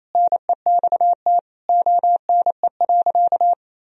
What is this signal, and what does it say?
Morse code, heard as a keyed tone, about 27 wpm, 700 Hz. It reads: NEXT ONE.
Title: Sentences from Top 400 Words and Abbreviations 28wpm